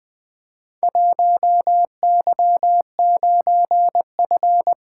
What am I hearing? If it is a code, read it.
Morse code, heard as a keyed tone, about 20 wpm, 700 Hz. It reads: 1Y9F
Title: Four Letter-Number Combinations - 20wpm